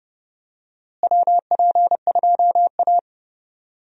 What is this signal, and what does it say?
Morse code, heard as a keyed tone, about 30 wpm, 700 Hz. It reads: WP2A
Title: USA Call Signs - 4 Characters - Repeated 3x - 30wpm